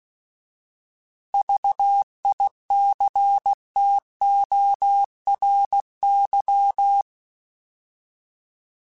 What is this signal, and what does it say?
Morse code, heard as a keyed tone, about 16 wpm, 800 Hz. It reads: VICTORY